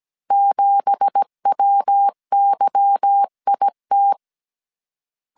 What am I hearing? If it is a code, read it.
Morse code, heard as a keyed tone, about 17 wpm, 800 Hz. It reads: 7WYIT
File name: CW / DET / 1 9 1500C S C 16wpm 2909